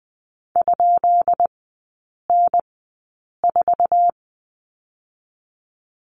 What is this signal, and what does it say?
Morse code, heard as a keyed tone, about 20 wpm, spaced as for 8 wpm, 700 Hz. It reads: ?N4
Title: 133 Lesson 22e - Question Mark - Three Characters 20wpm